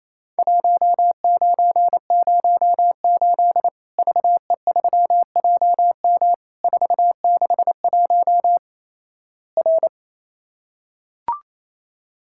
Morse code, heard as a keyed tone, about 28 wpm, 700 Hz. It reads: 1908 VE3JM 461 R
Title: Worked All Europe DX Contest - QTC Exchange - 28wpm